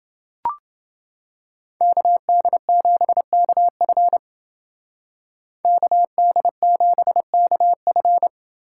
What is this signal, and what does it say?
Morse code, heard as a keyed tone, about 29 wpm, 700 Hz. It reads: KD7KF KD7KF